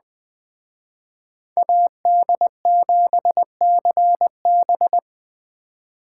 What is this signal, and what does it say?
Morse code, heard as a keyed tone, about 20 wpm, 700 Hz. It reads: AD7CB